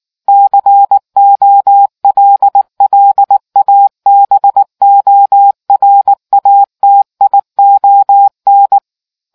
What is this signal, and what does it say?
Morse code, heard as a keyed tone, about 19 wpm, 800 Hz. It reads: COLLABORATION